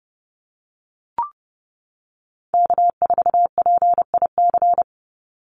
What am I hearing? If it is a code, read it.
Morse code, heard as a keyed tone, about 30 wpm, 700 Hz. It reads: K4PIC
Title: NA SOTA Chaser Callsigns 30wpm